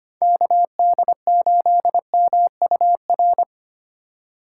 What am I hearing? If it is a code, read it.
Morse code, heard as a keyed tone, about 29 wpm, 700 Hz. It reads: KD8MUR
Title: Speed Racing - USA Call Signs 25wpm